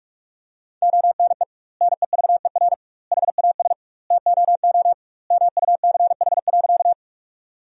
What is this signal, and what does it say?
Morse code, heard as a keyed tone, about 48 wpm, 700 Hz. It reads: ONE NEVER HAS TOO MUCH.
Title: 2x Word-Spacing - Sentences from Top 300 Words - 45wpm